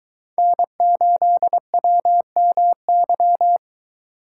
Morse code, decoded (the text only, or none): N8WMY